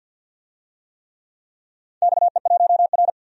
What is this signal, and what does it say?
Morse code, heard as a keyed tone, about 52 wpm, 700 Hz. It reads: XE1R
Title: DXCC-100 Call Signs - 4 Characters - 50wpm